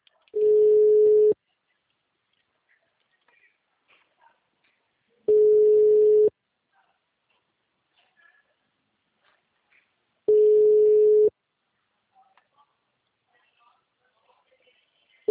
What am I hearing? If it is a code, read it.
Ringback tone (Europe, 425 Hz).
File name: out- -2011- - - .55149